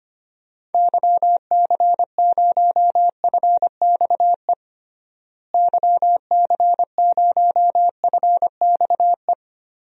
Morse code, heard as a keyed tone, about 25 wpm, 700 Hz. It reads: YC0FXE YC0FXE
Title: DXCC-100 Call Signs - 6 Characters - Repeated 2x - 25wpm